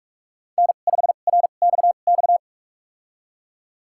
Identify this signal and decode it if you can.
Morse code, heard as a keyed tone, about 46 wpm, spaced as for 33 wpm, 700 Hz. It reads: N5RXX
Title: USA Call Signs - 5 Characters - 2x Character Spacing - 45wpm